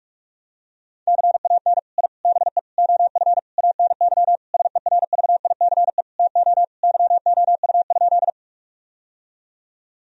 Morse code, heard as a keyed tone, about 42 wpm, 700 Hz. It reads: CAN I BE OF ANY SERVICE TO YOU?
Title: Sentences from Top 1000 Words - Review Part 4 - 45wpm